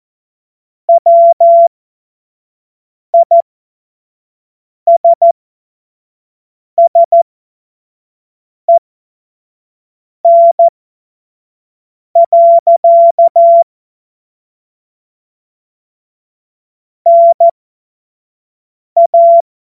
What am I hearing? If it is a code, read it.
Morse code, heard as a keyed tone, about 14 wpm, spaced as for 5 wpm, 680 Hz. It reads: WISSEN. NA
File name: Morsekurs / Witze / Witz 08. August 2023 25 F70